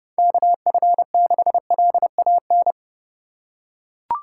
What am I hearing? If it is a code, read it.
Morse code, heard as a keyed tone, about 32 wpm, 700 Hz. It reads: KF6LAN
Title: USA Call Signs - 6 Characters - 4x Character Spacing - 30wpm